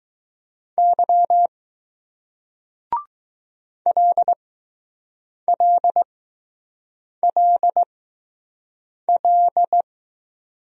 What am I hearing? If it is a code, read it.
Morse code, heard as a keyed tone, about 20 wpm, spaced as for 6 wpm, 700 Hz. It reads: Y LLLL